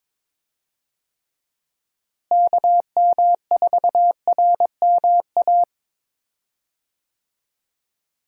Morse code, heard as a keyed tone, about 22 wpm, 700 Hz. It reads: KM4RMA
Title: USA Call Signs - 6 Characters - Repeated 2x - 22wpm